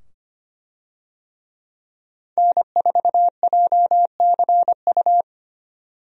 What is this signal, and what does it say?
Morse code, heard as a keyed tone, about 25 wpm, 700 Hz. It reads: N4JCU
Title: USA Call Signs - 5 Characters - 25wpm